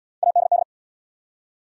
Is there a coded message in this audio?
Yes: Morse code from 0.2 to 0.6 s, heard as a keyed tone, about 78 wpm, 690 Hz. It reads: ADD